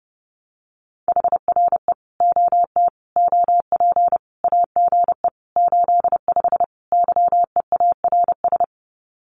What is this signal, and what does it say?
Morse code, heard as a keyed tone, about 28 wpm, 700 Hz. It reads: HRE OT OP AGE 85 YEARS
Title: Common QSO Phrases 30wpm